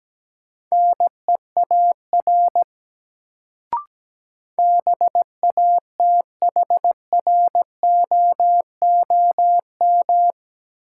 Morse code, heard as a keyed tone, about 17 wpm, 700 Hz. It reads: NEAR BATHROOM
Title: Sets of 5 Words Encore - New Word Drill - 17wpm